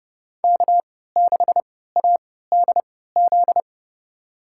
Morse code, heard as a keyed tone, about 30 wpm, spaced as for 17 wpm, 700 Hz. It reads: K6ADZ
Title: USA Call Signs - 3x Character Spacing - 30wpm